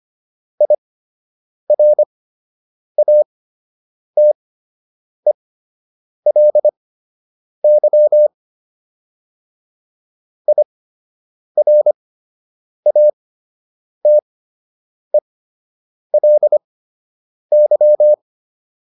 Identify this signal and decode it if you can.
Morse code, heard as a keyed tone, about 25 wpm, spaced as for 8 wpm, 600 Hz. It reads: IRATELY IRATELY